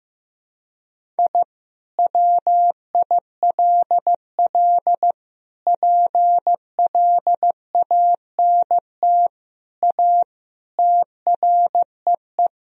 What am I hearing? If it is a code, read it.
Morse code, heard as a keyed tone, about 16 wpm, 700 Hz. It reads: I WILL PLANT A TREE
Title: Sentences from Top 1000 Words - Review Part 1 - 15wpm